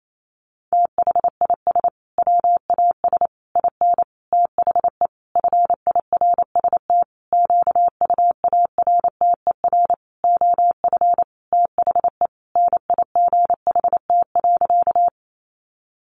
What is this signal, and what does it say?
Morse code, heard as a keyed tone, about 28 wpm, 700 Hz. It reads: THIS WAS IN THE FIRST QUARTER OF THE NIGHT.
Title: Sentences from Top 1300 Words 28wpm